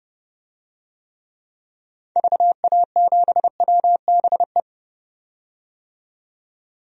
Morse code, heard as a keyed tone, about 30 wpm, 700 Hz. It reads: VA7WBE
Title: DXCC-100 Call Signs - 6 Characters - 30wpm